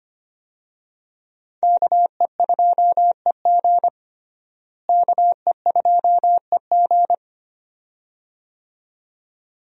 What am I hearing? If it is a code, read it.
Morse code, heard as a keyed tone, about 25 wpm, 700 Hz. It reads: KE2EG KE2EG
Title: USA Call Signs - 5 Characters - Repeated 2x - 25wpm